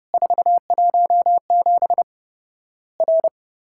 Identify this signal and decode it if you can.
Morse code, heard as a keyed tone, about 32 wpm, 700 Hz. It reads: 417 R
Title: Worked All Europe DX Contest - QTC Exchange - 30wpm